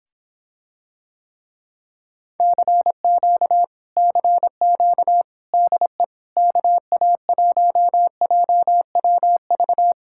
Morse code, heard as a keyed tone, about 26 wpm, 700 Hz. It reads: CQ CQ DE KA1JWV